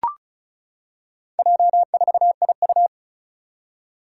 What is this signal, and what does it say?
Morse code, heard as a keyed tone, about 36 wpm, 700 Hz. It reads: J4IU
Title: Four Challenging Letter-Number Combinations 35wpm